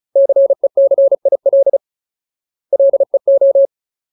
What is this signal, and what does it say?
Morse code, heard as a keyed tone, about 37 wpm, 550 Hz. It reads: CECIL LEO
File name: Ultimate Word List / 35WPM Ultimate